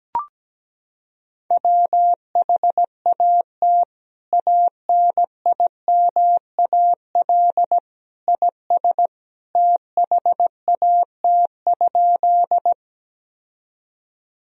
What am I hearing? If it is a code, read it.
Morse code, heard as a keyed tone, about 17 wpm, 700 Hz. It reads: WHAT ANIMAL IS THAT?
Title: Sentences from Top 1000 Words - Review Part 1 - 17wpm